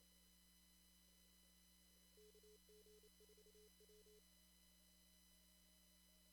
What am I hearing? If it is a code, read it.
Morse code, heard as a keyed tone, about 28 wpm, 420 Hz. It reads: KG4W
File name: CW / kt5j wpx cw-op-k5pi